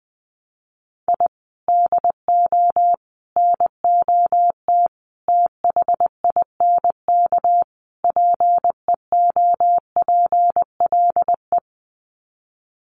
Morse code, heard as a keyed tone, about 20 wpm, 700 Hz. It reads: I DO NOT THINK PEOPLE
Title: Sets of 5 Words Encore2 20wpm